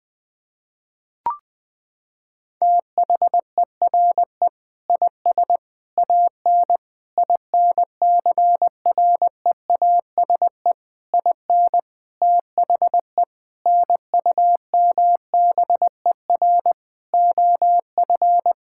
Morse code, heard as a keyed tone, about 20 wpm, 700 Hz. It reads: THERE IS AN INCREASE IN THE NUMBER OF
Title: Sentences with 5-Word Phrases - 20wpm